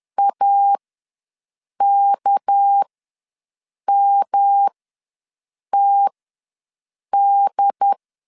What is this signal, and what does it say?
Morse code, heard as a keyed tone, about 10 wpm, spaced as for 6 wpm, 800 Hz. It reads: AKMTD
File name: LEZIONI / 6 PKX0 / 6 PK ws 10wpm 3745